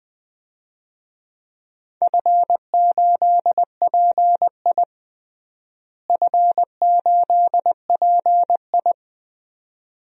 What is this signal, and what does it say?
Morse code, heard as a keyed tone, about 20 wpm, 700 Hz. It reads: F8PI F8PI
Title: DXCC-100 Call Signs - 4 Characters - Repeated 2x - 20wpm